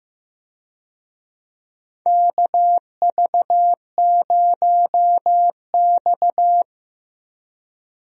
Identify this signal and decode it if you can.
Morse code, heard as a keyed tone, about 15 wpm, 700 Hz. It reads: KV0X
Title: USA Call Signs - 4 Characters - Repeated 2x - 15wpm